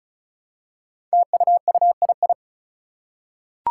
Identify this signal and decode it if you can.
Morse code, heard as a keyed tone, about 36 wpm, 700 Hz. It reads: TUUII
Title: Five Challenging Letter-Number Combinations 35wpm